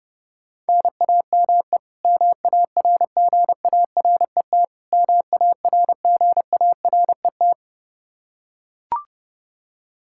Morse code, heard as a keyed tone, about 32 wpm, 700 Hz. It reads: NAME MARGARET MARGARET
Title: Practice QSO - Exchange 1 - Beginning - 30wpm